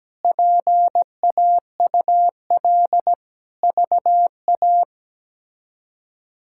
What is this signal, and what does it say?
Morse code, heard as a keyed tone, about 17 wpm, 700 Hz. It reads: PAUL VA
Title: K1USN SST QSO Practice - 17wpm